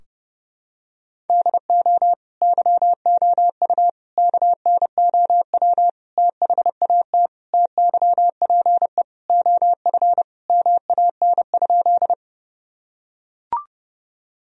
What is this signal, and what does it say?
Morse code, heard as a keyed tone, about 30 wpm, 700 Hz. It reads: DO YOU KNOW THAT TYPE OF MAN?
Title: Sentences from Top 1000 Words - Review Part 3 - 30wpm